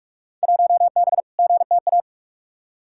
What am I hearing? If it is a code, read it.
Morse code, heard as a keyed tone, about 45 wpm, 700 Hz. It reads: 1B GTA
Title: Field Day Exchanges Encore - 45wpm